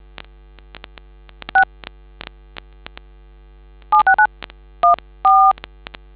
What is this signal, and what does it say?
DTMF, heard as telephone touch tones, tones of about 110 ms. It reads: 6 76914